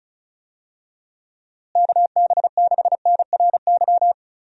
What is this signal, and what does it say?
Morse code, heard as a keyed tone, about 36 wpm, 700 Hz. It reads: KB6NRY
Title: USA Call Signs - 6 Characters - Repeated 3x - 35wpm